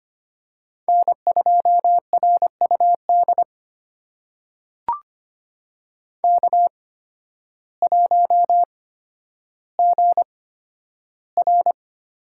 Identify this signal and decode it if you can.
Morse code, heard as a keyed tone, about 25 wpm, spaced as for 12 wpm, 700 Hz. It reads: N2RUD K1GR
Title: USA Call Signs - 8x Character Spacing - 25wpm